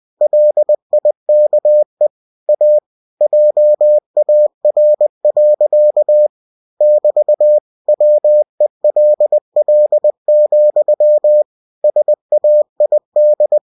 Morse code, heard as a keyed tone, about 20 wpm, 600 Hz. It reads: LIKE A JAR. <BT> WELL, SAID
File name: cw-books / winnie-the-pooh / winnie-20wpm-page-0043